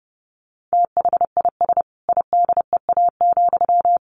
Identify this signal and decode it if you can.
Morse code, heard as a keyed tone, about 28 wpm, 700 Hz. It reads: THIS IDEA,
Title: Sentences from Top 1300 Words 30wpm